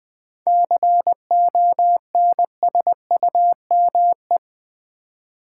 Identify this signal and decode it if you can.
Morse code, heard as a keyed tone, about 20 wpm, 700 Hz. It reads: CONSUME
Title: 7 Letter Words - First 3 Letters Given - 20wpm